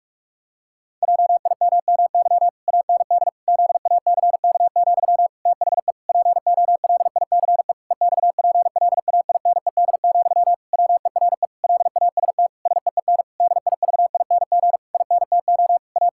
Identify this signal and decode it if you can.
Morse code, heard as a keyed tone, about 46 wpm, 700 Hz. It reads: JIMMY AND ZACK, THE POLICE EXPLAINED, WERE LAST SEEN DIVING INTO A
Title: Pangram Challenge v2 - 45wpm